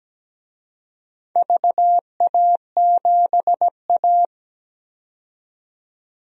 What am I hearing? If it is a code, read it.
Morse code, heard as a keyed tone, about 17 wpm, 700 Hz. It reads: VA7A
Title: DXCC-100 Call Signs - 4 Characters - 17wpm